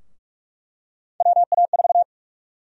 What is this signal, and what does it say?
Morse code, heard as a keyed tone, about 46 wpm, 700 Hz. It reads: WA4
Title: USA Call Sign Prefixes - 45wpm